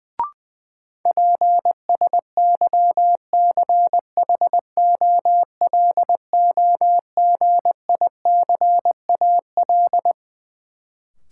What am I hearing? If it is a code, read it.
Morse code, heard as a keyed tone, about 21 wpm, 700 Hz. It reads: PSYCHOLOGICAL
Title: Sentences from Top 1400 Words - New Word Drill - 20wpm